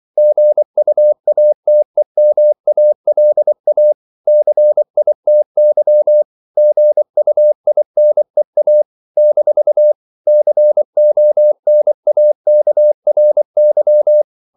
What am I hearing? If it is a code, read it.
Morse code, heard as a keyed tone, about 24 wpm, 600 Hz. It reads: GUATEMALA CITY GUINEA - CONAKRY
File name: CW / Practice / NationalCapitals24